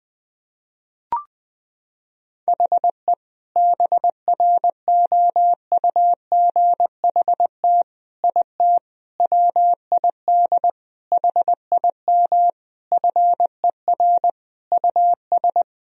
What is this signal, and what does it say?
Morse code, heard as a keyed tone, about 21 wpm, 700 Hz. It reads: HE BROUGHT IT WID HIM FER US
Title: Sentences from Top 500 Words and Abbreviations 20wpm